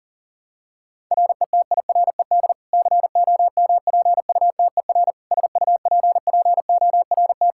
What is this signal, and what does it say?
Morse code, heard as a keyed tone, about 37 wpm, 700 Hz. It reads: RETIRED COMPUTER SUPPORT
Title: Practice QSO - Exchange 3 - More Info - 40wpm